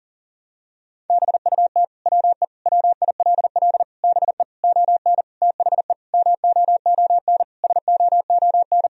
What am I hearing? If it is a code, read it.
Morse code, heard as a keyed tone, about 40 wpm, 700 Hz. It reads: BUT WE WILL BE ON THE MOON SOON
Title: Sentences from Top 1000 Words - Review Part 3 - 40wpm